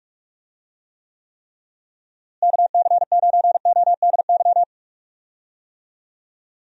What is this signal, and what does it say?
Morse code, heard as a keyed tone, about 44 wpm, 700 Hz. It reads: KC9ODY
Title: USA Call Signs - 6 Characters - Repeated 3x - 45wpm